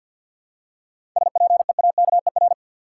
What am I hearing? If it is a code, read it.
Morse code, heard as a keyed tone, about 52 wpm, 700 Hz. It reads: SPEAKER
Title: Seven Letter Words - 50wpm